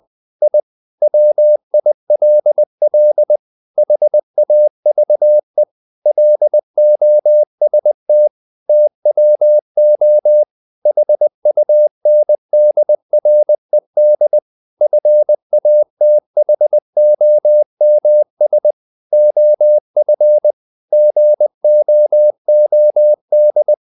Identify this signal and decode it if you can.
Morse code, heard as a keyed tone, about 20 wpm, 600 Hz. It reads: I WILL HAVE LOST TWO HUNDRED FATHOMS OF GOOD
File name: the old man and the sea / the old man and the sea part 1 20wpm